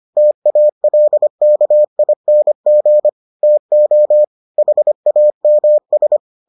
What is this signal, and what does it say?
Morse code, heard as a keyed tone, about 25 wpm, 600 Hz. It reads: TALKING TO HAMS